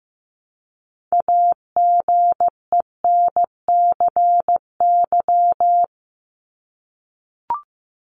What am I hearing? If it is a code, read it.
Morse code, heard as a keyed tone, about 15 wpm, 700 Hz. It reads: AGENCY
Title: Top 1001-1250 Words 15wpm